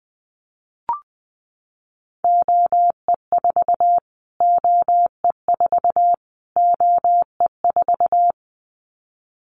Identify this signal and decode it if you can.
Morse code, heard as a keyed tone, about 20 wpm, 700 Hz. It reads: OE4 OE4 OE4
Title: Callsign Prefixes for Top 25 DXCC - Repeated 3x - 20wpm